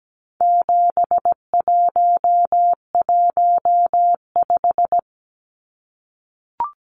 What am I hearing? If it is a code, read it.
Morse code, heard as a keyed tone, about 17 wpm, 700 Hz. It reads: 7115
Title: Numbers - 4-Digit 17wpm